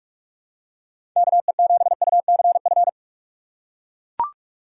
Morse code, heard as a keyed tone, about 42 wpm, 700 Hz. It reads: KE7UCF
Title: USA Call Signs - 2x Character Spacing - 45wpm